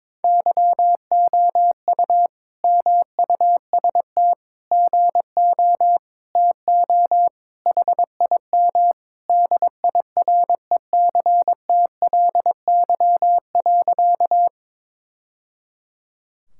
Morse code, heard as a keyed tone, about 22 wpm, 700 Hz. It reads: YOU MUST GO TO HIM DIRECTLY.
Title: Sentences from Top 1000 Words - Review Part 5 - 22wpm